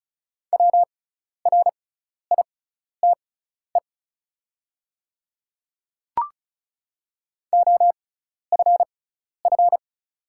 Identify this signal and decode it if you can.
Morse code, heard as a keyed tone, about 34 wpm, spaced as for 12 wpm, 690 Hz. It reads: WRITE OFF
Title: 6x Character Spacing - Top 200 Words - 35wpm